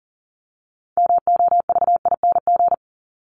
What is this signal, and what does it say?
Morse code, heard as a keyed tone, about 40 wpm, 700 Hz. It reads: MOVING
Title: ING Suffix - Top 100 Words with Base Words 40wpm